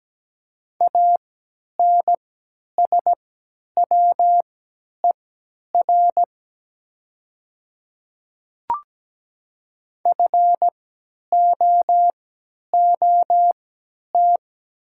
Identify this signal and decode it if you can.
Morse code, heard as a keyed tone, about 17 wpm, spaced as for 10 wpm, 700 Hz. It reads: ANSWER FOOT
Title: 3x Character Spacing - Top 500 Words - 17wpm